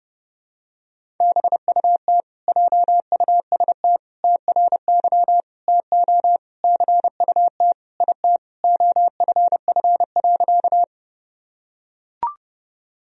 Morse code, heard as a keyed tone, about 30 wpm, 700 Hz. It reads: BUT JUST TRY TO CUT IT OFF.
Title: Sentences from Top 1000 Words - Review Part 1 - 30wpm